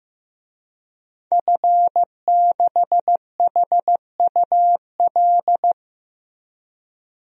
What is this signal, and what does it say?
Morse code, heard as a keyed tone, about 16 wpm, 700 Hz. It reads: F6HUL
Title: DXCC-100 Call Signs - 5 Characters - Repeated 2x - 15wpm